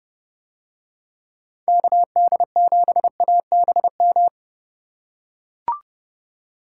Morse code, heard as a keyed tone, about 31 wpm, 700 Hz. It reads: KD7ABM